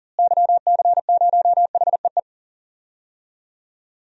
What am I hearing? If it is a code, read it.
Morse code, heard as a keyed tone, about 40 wpm, 700 Hz. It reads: YC0HEE
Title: DXCC-100 Call Signs - 6 Characters - Repeated 2x - 40wpm